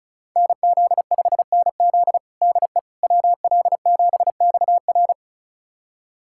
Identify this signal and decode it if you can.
Morse code, heard as a keyed tone, about 34 wpm, 700 Hz. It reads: NZ5NZ DE WL7XR
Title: USA Callsign DE Callsign - 5 Characters - 35wpm